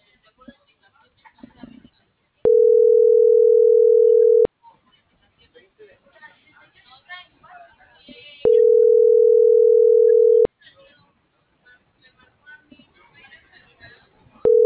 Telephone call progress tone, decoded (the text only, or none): ringback tone